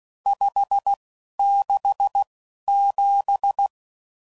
Morse code, heard as a keyed tone, about 16 wpm, spaced as for 11 wpm, 800 Hz. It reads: 567
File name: CW Code Groups @10WPM / RSTs / Various RST No's